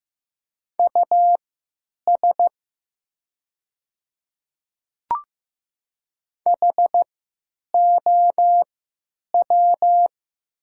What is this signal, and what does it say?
Morse code, heard as a keyed tone, about 15 wpm, spaced as for 9 wpm, 700 Hz. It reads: US HOW
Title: 3x Character Spacing - Top 200 Words - 15wpm